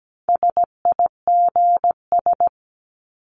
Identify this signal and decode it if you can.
Morse code, heard as a keyed tone, about 17 wpm, 700 Hz. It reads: SIGS
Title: Abbreviations 17wpm